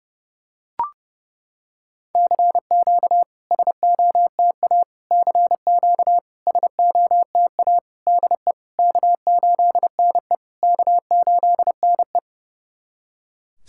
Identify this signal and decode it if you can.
Morse code, heard as a keyed tone, about 30 wpm, 700 Hz. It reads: CQ SOTA CQ SOTA DE K8NE K8NE
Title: SOTA QSO Practice 30wpm